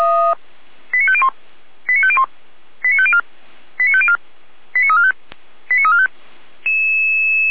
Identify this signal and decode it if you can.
Five-tone selective call (ZVEI), heard as quick runs of five tones, tones of about 70 ms. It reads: 89501 89501 89504 89504 89345 89345